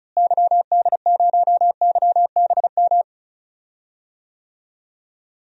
Morse code, heard as a keyed tone, about 36 wpm, 700 Hz. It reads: YD0YBM
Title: DXCC-100 Call Signs - 6 Characters - Repeated 3x - 35wpm